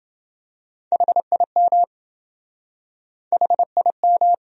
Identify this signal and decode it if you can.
Morse code, heard as a keyed tone, about 28 wpm, 700 Hz. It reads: HIM HIM